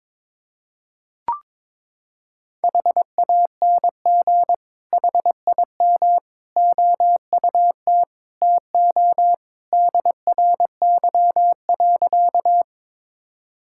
Morse code, heard as a keyed tone, about 21 wpm, 700 Hz. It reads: HANG HIM OUT TO DRY.
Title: Sentences from Top 1600 Words 22wpm